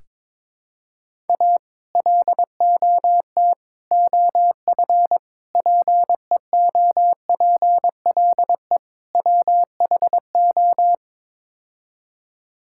Morse code, heard as a keyed tone, about 22 wpm, 700 Hz. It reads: A LOT OF PEOPLE WHO
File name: practice / 22wpm / Build A Phrase - 5-Word Phrases - 22wpm